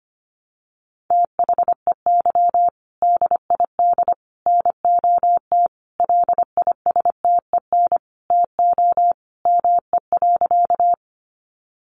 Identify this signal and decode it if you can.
Morse code, heard as a keyed tone, about 25 wpm, 700 Hz. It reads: THEY DID NOT LISTEN TO ME.
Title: Sentences from Top 600 Words 25wpm